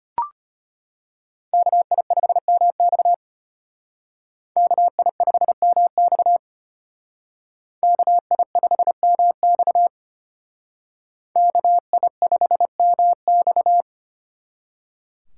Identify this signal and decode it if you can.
Morse code, heard as a keyed tone, about 32 wpm, 700 Hz. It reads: KI5MX KI5MX KI5MX KI5MX